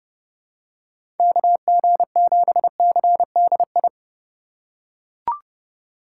Morse code, heard as a keyed tone, about 32 wpm, 700 Hz. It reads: KG7CDI
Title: USA Call Signs - 6 Characters - 2x Character Spacing - 30wpm